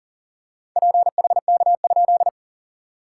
Morse code, heard as a keyed tone, about 39 wpm, 690 Hz. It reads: PHK?
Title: SOTA QSO Practice 40wpm